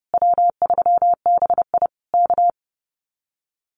Morse code, heard as a keyed tone, about 30 wpm, 700 Hz. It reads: W3BI K
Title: Common QSO Phrases Encore 30wpm